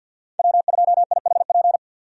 Morse code, heard as a keyed tone, about 52 wpm, 700 Hz. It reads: W2IHP
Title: USA Call Signs - 5 Characters - 50wpm